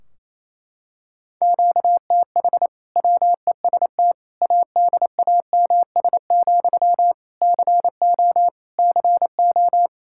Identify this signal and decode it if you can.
Morse code, heard as a keyed tone, about 28 wpm, 700 Hz. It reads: QTH WEST ADAMS, CO CO
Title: Practice QSO - Exchange 1 - Beginning - 28wpm